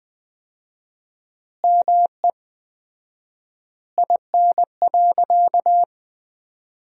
Morse code, heard as a keyed tone, about 21 wpm, 700 Hz. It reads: ME IN.
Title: 4x Word-Spacing - Sentences from Top 500 Words - 20wpm